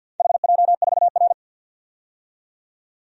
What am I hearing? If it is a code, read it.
Morse code, heard as a keyed tone, about 47 wpm, 700 Hz. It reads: HJ4R
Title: DXCC-100 Call Signs - 4 Characters - 50wpm